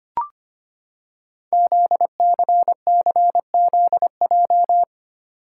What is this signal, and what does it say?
Morse code, heard as a keyed tone, about 25 wpm, 700 Hz. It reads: ZCCZJ
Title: Code Groups - 5 Letters - 25wpm